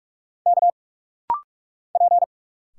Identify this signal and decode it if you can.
Morse code, heard as a keyed tone, about 45 wpm, 700 Hz. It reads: K P
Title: Single Letters - Rapid-Fire 45wpm